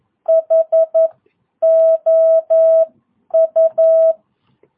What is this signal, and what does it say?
Morse code, heard as a keyed tone, about 11 wpm, spaced as for 9 wpm, 650 Hz. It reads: HOU
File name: jh8cbh / 2022cw